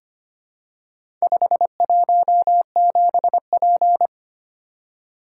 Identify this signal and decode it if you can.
Morse code, heard as a keyed tone, about 25 wpm, 700 Hz. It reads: 517P
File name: practice / 25wpm / Four Letter-Number Combinations - 25wpm